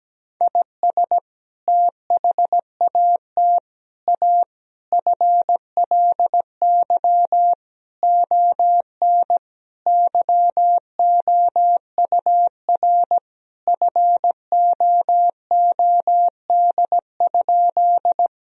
Morse code, heard as a keyed tone, about 17 wpm, 700 Hz. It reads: IS THAT A FLY ON YOUR FOOD?
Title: Sentences from Top 1000 Words - Review Part 2 - 17wpm